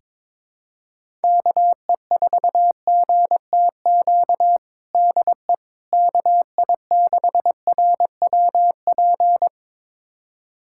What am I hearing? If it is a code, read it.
Morse code, heard as a keyed tone, about 22 wpm, 700 Hz. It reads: KE4GTQ DE KI6RWP
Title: USA Callsign DE Callsign - 6 Characters - 22wpm